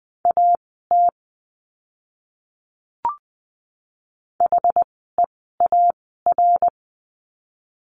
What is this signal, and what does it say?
Morse code, heard as a keyed tone, about 20 wpm, spaced as for 14 wpm, 700 Hz. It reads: AT HEAR
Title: 206 Lesson 33f - Letter j - Words 20wpm